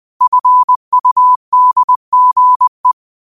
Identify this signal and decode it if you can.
Morse code, heard as a keyed tone, about 20 wpm, 1000 Hz. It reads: FUDGE
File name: qotdfiles / 20wpm